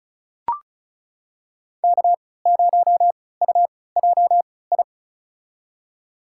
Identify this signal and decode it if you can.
Morse code, heard as a keyed tone, about 36 wpm, spaced as for 20 wpm, 700 Hz. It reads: K0UJI
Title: USA Call Signs - 5 Characters - 3x Character Spacing - 35wpm